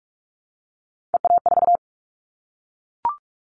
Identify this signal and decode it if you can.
Morse code, heard as a keyed tone, about 44 wpm, 700 Hz. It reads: EA4
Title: Callsign Prefixes for Top 25 DXCC - Repeated 3x - 45wpm